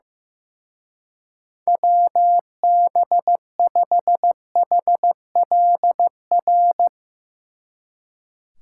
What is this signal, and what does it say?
Morse code, heard as a keyed tone, about 16 wpm, 700 Hz. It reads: WB5HLR
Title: Speed Racing - USA 2x3 Call Signs 15wpm